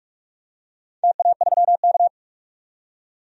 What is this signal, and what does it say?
Morse code, heard as a keyed tone, about 46 wpm, 700 Hz. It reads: TA3K
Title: DXCC-100 Call Signs - 4 Characters - 45wpm